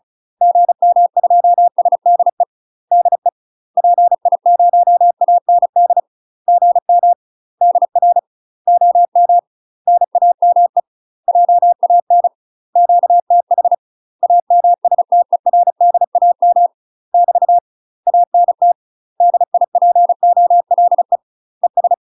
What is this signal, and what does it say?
Morse code, heard as a keyed tone, about 35 wpm, 700 Hz. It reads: GM2SDE DE PI0AND GM DR OM NAME JAN QTH AMSTERDAM <BT> ANT DIPOLE ES
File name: UK MASTER QSOs TRAINING / 35WPM / UK MASTER QSOs 0017